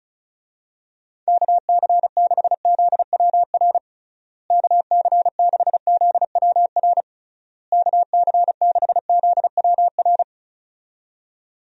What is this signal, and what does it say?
Morse code, heard as a keyed tone, about 36 wpm, 700 Hz. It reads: KC6ZWR KC6ZWR KC6ZWR
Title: USA Call Signs - 6 Characters - Repeated 3x - 35wpm